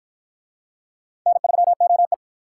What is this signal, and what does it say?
Morse code, heard as a keyed tone, about 53 wpm, 700 Hz. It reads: N3OE